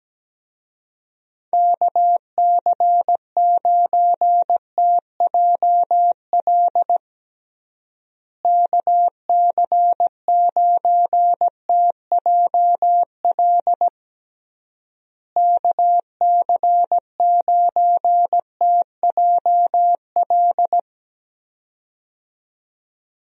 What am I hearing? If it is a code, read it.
Morse code, heard as a keyed tone, about 17 wpm, 700 Hz. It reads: KC9TJL KC9TJL KC9TJL
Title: USA Call Signs - 6 Characters - Repeated 3x - 17wpm